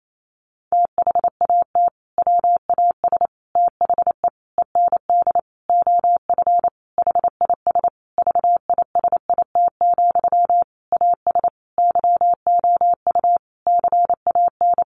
Morse code, heard as a keyed tone, about 28 wpm, 700 Hz. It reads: THAT WAS THE END OF HIS VISIT, AS YOU CAN
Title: Sentences from Top 1400 Words 28wpm